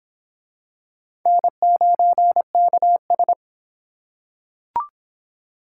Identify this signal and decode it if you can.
Morse code, heard as a keyed tone, about 25 wpm, 700 Hz. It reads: N9KS